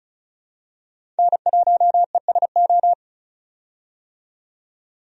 Morse code, heard as a keyed tone, about 34 wpm, 700 Hz. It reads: N1ESO
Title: USA Call Signs - 5 Characters - 35wpm